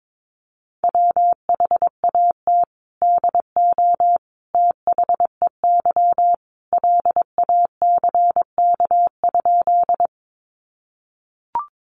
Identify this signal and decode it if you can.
Morse code, heard as a keyed tone, about 22 wpm, 700 Hz. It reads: WHAT DO THEY LACK?
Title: Sentences from Top 1100 Words 22wpm